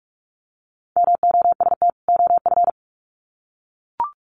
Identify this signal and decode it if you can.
Morse code, heard as a keyed tone, about 44 wpm, 700 Hz. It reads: MOST OF
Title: Sets of 2 Words 45wpm